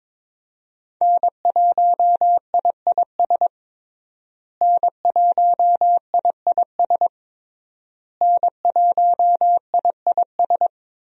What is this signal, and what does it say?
Morse code, heard as a keyed tone, about 22 wpm, 700 Hz. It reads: N1IIS N1IIS N1IIS
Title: USA Call Signs - 5 Characters - Repeated 3x - 22wpm